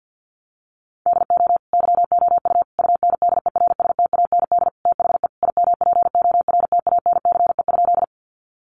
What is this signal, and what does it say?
Morse code, heard as a keyed tone, about 50 wpm, 700 Hz. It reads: DO YOU UNDERSTAND THE IMPORTANCE?
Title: Sentences from Top 1200 Words 50wpm